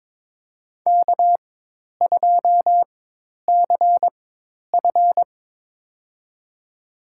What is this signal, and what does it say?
Morse code, heard as a keyed tone, about 22 wpm, spaced as for 10 wpm, 700 Hz. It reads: K2CF
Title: USA Call Signs - 4 Characters - 4x Character Spacing - 22wpm